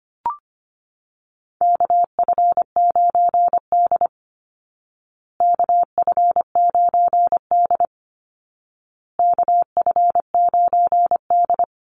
Morse code, heard as a keyed tone, about 25 wpm, 700 Hz. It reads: KF9D KF9D KF9D
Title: Call Signs Repeated 2x - 25wpm